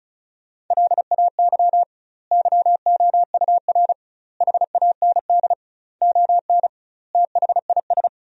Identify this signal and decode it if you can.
Morse code, heard as a keyed tone, about 34 wpm, 700 Hz. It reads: LAY YOUR HAND ON THIS
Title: 2x Word-Spacing - Sentences from Top 400 Words - 35wpm